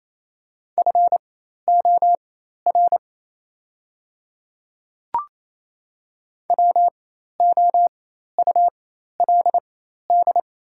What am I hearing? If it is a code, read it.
Morse code, heard as a keyed tone, about 27 wpm, spaced as for 13 wpm, 700 Hz. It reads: FOR WOULD
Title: 4x Character Spacing - Top 200 Words - 28wpm